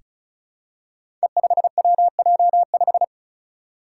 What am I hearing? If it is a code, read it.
Morse code, heard as a keyed tone, about 36 wpm, 700 Hz. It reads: E5WJ5
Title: Five Challenging Letter-Number Combinations 35wpm